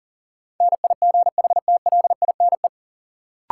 Morse code, heard as a keyed tone, about 40 wpm, 700 Hz. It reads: NIGHTLINE
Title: Sets of 5 Words - New Word Drill - 40wpm